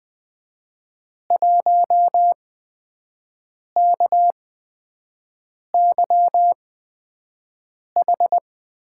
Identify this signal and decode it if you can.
Morse code, heard as a keyed tone, about 21 wpm, spaced as for 5 wpm, 700 Hz. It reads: 1KYH
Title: USA Call Signs - 8x Character Spacing - 20wpm